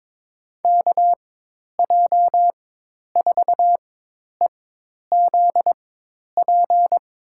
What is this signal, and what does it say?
Morse code, heard as a keyed tone, about 22 wpm, spaced as for 10 wpm, 700 Hz. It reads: KJ4EZP
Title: USA Call Signs - 6 Characters - 4x Character Spacing - 22wpm